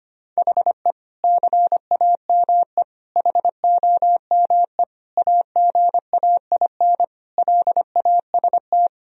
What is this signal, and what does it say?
Morse code, heard as a keyed tone, about 25 wpm, 700 Hz. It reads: HE CAME HOME AGAIN LAST
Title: Sentences from Top 1000 Words - Review Part 2 - 25wpm